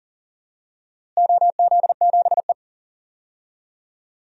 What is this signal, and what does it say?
Morse code, heard as a keyed tone, about 40 wpm, 700 Hz. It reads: OZ7E
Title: DXCC-100 Call Signs - 4 Characters - Repeated 3x - 40wpm